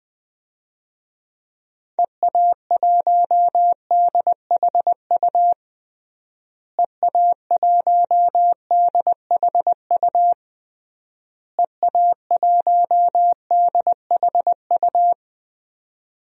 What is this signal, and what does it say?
Morse code, heard as a keyed tone, about 20 wpm, 700 Hz. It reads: EA1DHU EA1DHU EA1DHU
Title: DXCC-100 Call Signs - 6 Characters - Repeated 3x - 20wpm